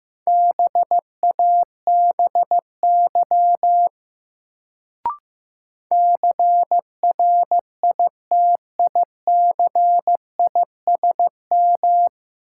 Morse code, heard as a keyed tone, about 15 wpm, 700 Hz. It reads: BABY CRITICISM
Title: Sentences from Top 1300 Words - New Word Drill - 15wpm